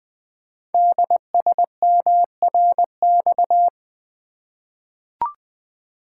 Morse code, heard as a keyed tone, about 20 wpm, 700 Hz. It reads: DSMRX
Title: Code Groups - 5 Letters - 20wpm